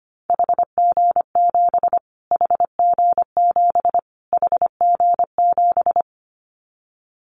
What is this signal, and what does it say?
Morse code, heard as a keyed tone, about 25 wpm, 700 Hz. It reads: HG7 HG7 HG7
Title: Callsign Prefixes for Top 25 DXCC - Repeated 3x - 25wpm